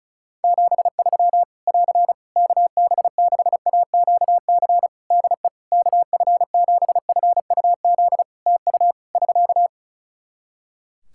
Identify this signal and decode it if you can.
Morse code, heard as a keyed tone, about 34 wpm, 700 Hz. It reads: 73 <AR> KB6AQC DE KF7FUZ TU <SK>
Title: Practice QSO - Exchange 4 - Ending - 35wpm